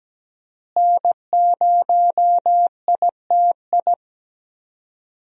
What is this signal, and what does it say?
Morse code, heard as a keyed tone, about 17 wpm, 700 Hz. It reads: N0ITI
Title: USA Call Signs - 5 Characters - Repeated 2x - 17wpm